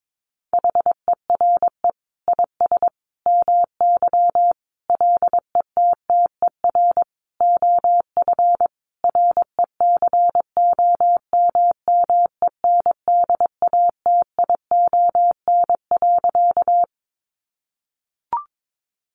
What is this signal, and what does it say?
Morse code, heard as a keyed tone, about 22 wpm, 700 Hz. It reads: HERE IS MY LETTER OF RECOMMENDATION.
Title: Sentences from Top 1400 Words 22wpm